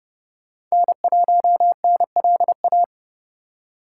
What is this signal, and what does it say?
Morse code, heard as a keyed tone, about 40 wpm, 700 Hz. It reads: N1NLA